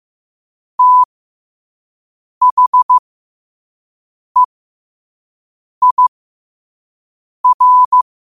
Morse code, heard as a keyed tone, about 15 wpm, spaced as for 5 wpm, 1000 Hz. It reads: THEIR
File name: qotdfiles / 5wpm